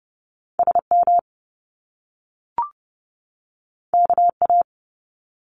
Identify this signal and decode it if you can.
Morse code, heard as a keyed tone, about 30 wpm, 700 Hz. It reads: SM KA
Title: Top Two Letter Combinations in Words 30wpm